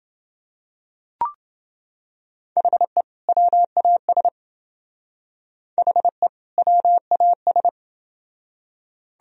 Morse code, heard as a keyed tone, about 27 wpm, 700 Hz. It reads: HE WAS HE WAS